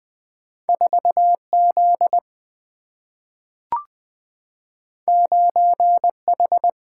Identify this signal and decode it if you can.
Morse code, heard as a keyed tone, about 20 wpm, 700 Hz. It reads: 4Z 9H
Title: Single-Letter Single-Number 20wpm